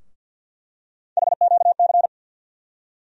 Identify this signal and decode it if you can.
Morse code, heard as a keyed tone, about 49 wpm, 700 Hz. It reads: HQC
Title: USA Call Sign Postfixes - 3 Letters 50wpm